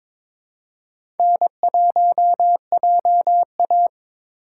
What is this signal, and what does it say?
Morse code, heard as a keyed tone, about 22 wpm, 700 Hz. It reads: N1JA
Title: USA Call Signs - 3x Character Spacing - 22wpm